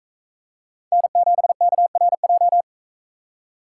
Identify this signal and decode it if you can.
Morse code, heard as a keyed tone, about 40 wpm, 700 Hz. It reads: N7KRJ